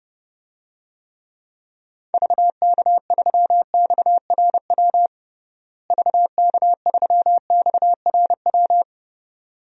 Morse code, heard as a keyed tone, about 29 wpm, 700 Hz. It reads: VK3XRW VK3XRW
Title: DXCC-100 Call Signs - 6 Characters - Repeated 2x - 30wpm